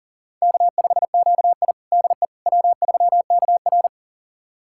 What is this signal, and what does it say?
Morse code, heard as a keyed tone, about 40 wpm, 700 Hz. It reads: K5QI DE W3KR
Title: USA Callsign DE Callsign - 4 Characters - 40wpm